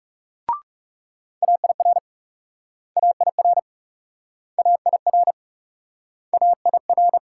Morse code, heard as a keyed tone, about 38 wpm, 700 Hz. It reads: AIR AIR AIR AIR